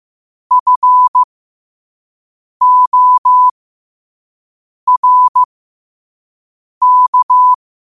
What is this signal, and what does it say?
Morse code, heard as a keyed tone, about 15 wpm, spaced as for 5 wpm, 1000 Hz. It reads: FORK
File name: qotdfiles / 5wpm